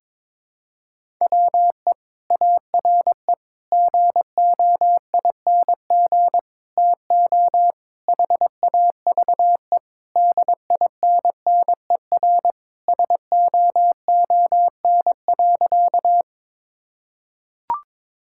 Morse code, heard as a keyed tone, about 22 wpm, 700 Hz. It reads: WE ARE GOING TO HAVE DINNER SOON.
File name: practice / 22wpm / Sentences with 3-Word Phrases - 22wpm